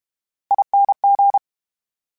Morse code, heard as a keyed tone, about 32 wpm, 800 Hz. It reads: ING
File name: Special INGs / Special INGs-mp3s / VerySlowToFaster Ing @30